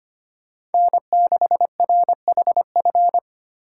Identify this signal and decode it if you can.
Morse code, heard as a keyed tone, about 32 wpm, 700 Hz. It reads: N6RHF